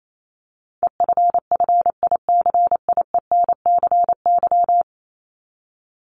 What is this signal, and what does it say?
Morse code, heard as a keyed tone, about 28 wpm, 700 Hz. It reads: EFFICIENCY
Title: Top 1501-1750 Words 28wpm